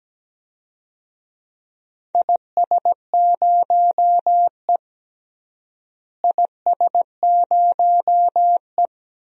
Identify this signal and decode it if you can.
Morse code, heard as a keyed tone, about 17 wpm, 700 Hz. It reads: IS0E IS0E
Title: DXCC-100 Call Signs - 4 Characters - Repeated 2x - 17wpm